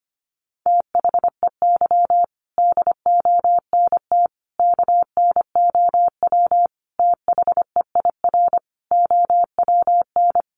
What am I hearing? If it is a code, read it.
Morse code, heard as a keyed tone, about 25 wpm, 700 Hz. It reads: THEY DONT KNOW THEIR OWN
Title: Sentences from Top 300 Words 25wpm